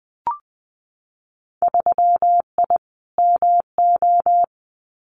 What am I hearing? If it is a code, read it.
Morse code, heard as a keyed tone, about 20 wpm, 700 Hz. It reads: 3I MO
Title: Winter Field Day Exchanges 20wpm